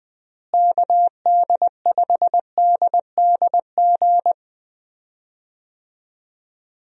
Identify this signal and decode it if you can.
Morse code, heard as a keyed tone, about 20 wpm, 700 Hz. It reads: KD5DDG
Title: USA Call Signs - 6 Characters - Repeated 2x - 20wpm